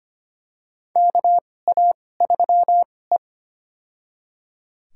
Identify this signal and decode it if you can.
Morse code, heard as a keyed tone, about 25 wpm, spaced as for 18 wpm, 700 Hz. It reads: KA3E
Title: USA Call Signs - 2x Character Spacing - 25wpm